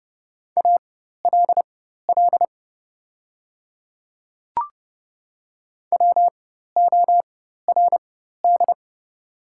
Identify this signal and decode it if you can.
Morse code, heard as a keyed tone, about 30 wpm, spaced as for 14 wpm, 700 Hz. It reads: ALL WORD
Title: 4x Character Spacing - Top 300 Words - 30wpm